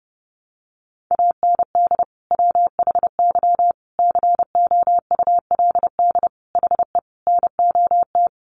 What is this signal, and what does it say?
Morse code, heard as a keyed tone, about 28 wpm, 700 Hz. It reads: AND WHY COULD HE NOT
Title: Sentences from Top 600 Words 30wpm